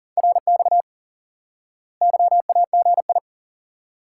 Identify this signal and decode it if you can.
Morse code, heard as a keyed tone, about 40 wpm, 700 Hz. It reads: RX YAGI
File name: IQE Course 3 - Lesson 89 - YAGI - Two Elements - 40wpm